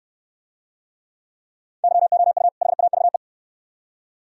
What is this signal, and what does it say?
Morse code, heard as a keyed tone, about 69 wpm, 700 Hz. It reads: YOU HAVE